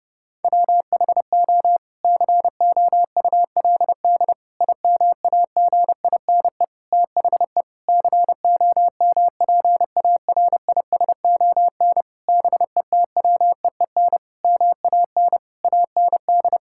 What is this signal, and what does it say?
Morse code, heard as a keyed tone, about 32 wpm, 700 Hz. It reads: WHO COULD IMAGINE THE COMPARISON BETWEEN MAN AND
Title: Sentences from Top 1600 Words 30wpm